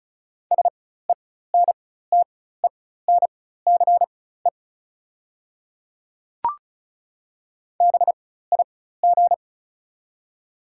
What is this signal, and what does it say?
Morse code, heard as a keyed tone, about 34 wpm, spaced as for 16 wpm, 690 Hz. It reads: SENTENCE BIG
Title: 4x Character Spacing - Top 200 Words - 35wpm